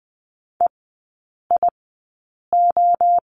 Morse code, heard as a keyed tone, about 20 wpm, 700 Hz. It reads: E I O
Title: 029 Lesson 6e - Letter i - Three Characters 20wpm